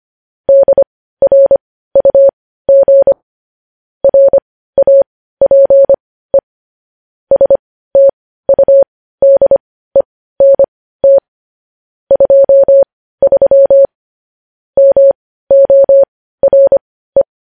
Morse code, heard as a keyed tone, about 25 wpm, spaced as for 15 wpm, 550 Hz. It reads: DRUG RAPE STUDENT 23 MORE
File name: podcast / News headlines-1x-25x15